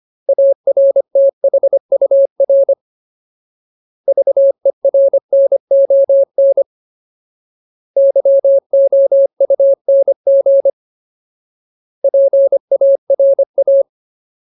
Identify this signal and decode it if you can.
Morse code, heard as a keyed tone, about 25 wpm, 550 Hz. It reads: ARTHUR VERNON YOUNG PARA
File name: Ultimate Word List / 25WPM Ultimate